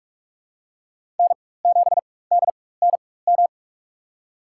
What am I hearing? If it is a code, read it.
Morse code, heard as a keyed tone, about 46 wpm, spaced as for 21 wpm, 700 Hz. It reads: N7DNM
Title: USA Call Signs - 5 Characters - 4x Character Spacing - 45wpm